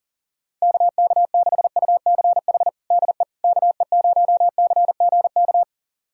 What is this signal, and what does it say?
Morse code, heard as a keyed tone, about 40 wpm, 700 Hz. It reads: KK6UCH DE KE0CGK
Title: USA Callsign DE Callsign 40wpm